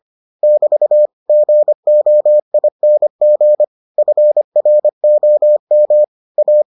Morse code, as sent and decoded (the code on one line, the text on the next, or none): -...- / --. --- .. -. --. / ..-. .-. --- -- / .-
<BT> GOING FROM A